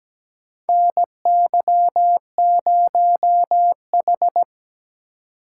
Morse code, heard as a keyed tone, about 17 wpm, 700 Hz. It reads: NY0H
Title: USA Call Signs - 4 Characters - 4x Character Spacing - 17wpm